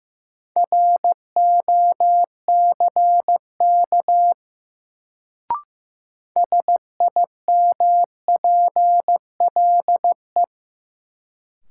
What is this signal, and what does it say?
Morse code, heard as a keyed tone, about 15 wpm, 700 Hz. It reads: ROCK SIMPLE
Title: Sentences from Top 400 Words - New Word Drill - 15wpm